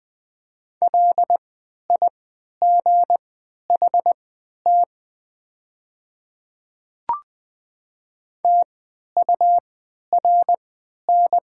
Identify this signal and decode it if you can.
Morse code, heard as a keyed tone, about 20 wpm, spaced as for 11 wpm, 700 Hz. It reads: LIGHT TURN
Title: 3x Character Spacing - Top 200 Words - 20wpm